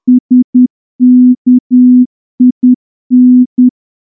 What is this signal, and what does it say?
Morse code, heard as a keyed tone, about 10 wpm, 250 Hz. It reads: SKIN
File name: morse-training-10WPM